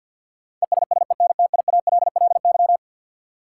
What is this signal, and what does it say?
Morse code, heard as a keyed tone, about 52 wpm, 700 Hz. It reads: ESSENTIALLY
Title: Sentences from Top 1300 Words - New Word Drill - 50wpm